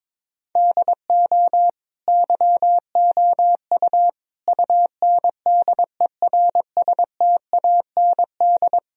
Morse code, heard as a keyed tone, about 22 wpm, 700 Hz. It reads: DO YOU UNDERSTAND
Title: Sentences from Top 1000 Words - Review Part 3 - 22wpm